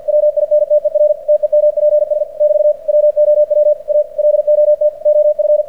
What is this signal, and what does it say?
Morse code, heard as a keyed tone, about 46 wpm, 600 Hz. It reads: WINTER TEMPS <BT> WOW A LOT OF
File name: archived / CRSnet